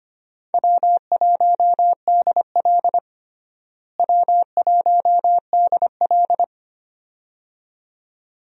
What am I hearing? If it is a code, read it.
Morse code, heard as a keyed tone, about 25 wpm, 700 Hz. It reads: W1DL W1DL
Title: USA Call Signs - 4 Characters - Repeated 2x - 25wpm